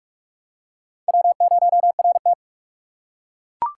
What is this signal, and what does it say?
Morse code, heard as a keyed tone, about 42 wpm, 700 Hz. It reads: W0RT
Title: USA Call Signs - 4 Characters - 2x Character Spacing - 45wpm